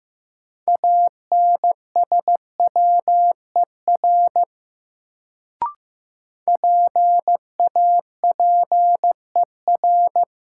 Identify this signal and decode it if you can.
Morse code, heard as a keyed tone, about 15 wpm, 700 Hz. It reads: ANSWER PAPER
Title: Sentences from Top 300 Words - New Word Drill - 15wpm